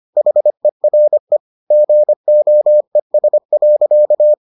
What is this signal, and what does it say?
Morse code, heard as a keyed tone, about 25 wpm, 600 Hz. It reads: HERE GOES.